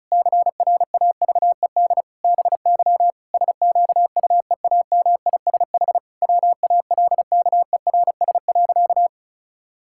Morse code, heard as a keyed tone, about 34 wpm, 700 Hz. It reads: CRAVED BY SQUEAMISH WALKERS.
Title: Pangram Challenge v2 - 35wpm